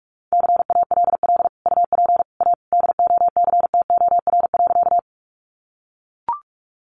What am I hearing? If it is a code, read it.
Morse code, heard as a keyed tone, about 44 wpm, 700 Hz. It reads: CALL UP A DOCTOR.
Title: Sentences from Top 1500 Words 45wpm